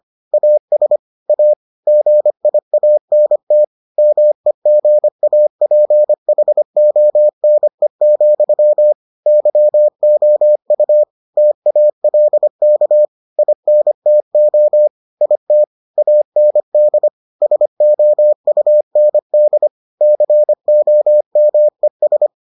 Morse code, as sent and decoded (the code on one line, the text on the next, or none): .- ... / .- / --. .. .- -. - / -- . --. .- .--. .... --- -. --..-- / -.-- --- ..- / - .- .-.. -.- / .. -. - --- / .. - / .- -. -.. / ... --- ..- -. -.. / -.-. --- -- . ...
AS A GIANT MEGAPHONE, YOU TALK INTO IT AND SOUND COMES